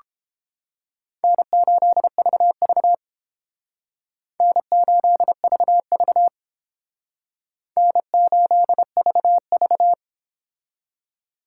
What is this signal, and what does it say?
Morse code, heard as a keyed tone, about 30 wpm, 700 Hz. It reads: N8VV N8VV N8VV